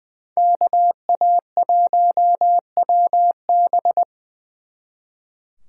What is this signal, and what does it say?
Morse code, heard as a keyed tone, about 23 wpm, 700 Hz. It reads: KA1WB
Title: Speed Racing - USA 2x2 Call Signs 20wpm